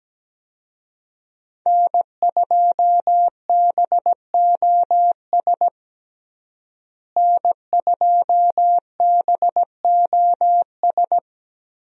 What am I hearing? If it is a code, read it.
Morse code, heard as a keyed tone, about 17 wpm, 700 Hz. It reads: N2BOS N2BOS
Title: USA Call Signs - 5 Characters - Repeated 3x - 17wpm